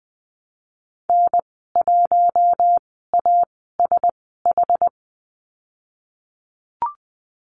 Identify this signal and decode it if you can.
Morse code, heard as a keyed tone, about 19 wpm, spaced as for 14 wpm, 700 Hz. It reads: N1ASH
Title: 082 Lesson 14g - Number 2 - Call Signs 20wpm